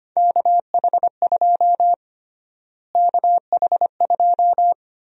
Morse code, heard as a keyed tone, about 25 wpm, 700 Hz. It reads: KH2 KH2
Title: USA Call Sign Prefixes - Repeated 2x - 25wpm